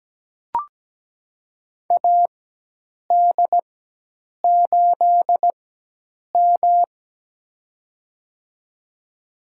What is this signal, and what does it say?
Morse code, heard as a keyed tone, about 17 wpm, spaced as for 8 wpm, 700 Hz. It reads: AD8M
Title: USA Call Signs - 4x Character Spacing - 17wpm